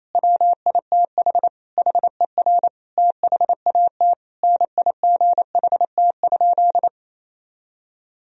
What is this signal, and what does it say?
Morse code, heard as a keyed tone, about 27 wpm, 700 Hz. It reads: WITH HER THAT NIGHT?
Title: Sentences from Top 1000 Words - Review Part 2 - 28wpm